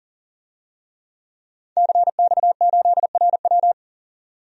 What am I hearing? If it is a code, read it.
Morse code, heard as a keyed tone, about 38 wpm, 700 Hz. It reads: CX8RW
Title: DXCC-100 Call Signs - 5 Characters - 40wpm